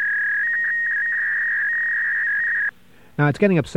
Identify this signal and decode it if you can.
Fax machine handshake, with speech.